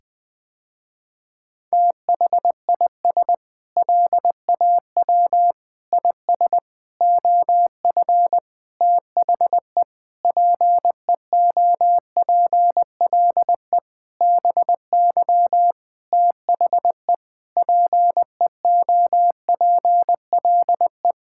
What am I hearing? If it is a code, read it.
Morse code, heard as a keyed tone, about 20 wpm, 700 Hz. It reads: THIS LAW IS OF THE PEOPLE BY THE PEOPLE
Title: Sentences with 9-Word Phrases - Spoken Beginning - 20wpm